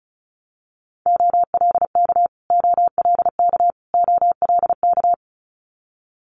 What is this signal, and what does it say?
Morse code, heard as a keyed tone, about 36 wpm, 700 Hz. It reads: OLK OLK OLK
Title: Multi-letter Phonemes Repeated 3x 35wpm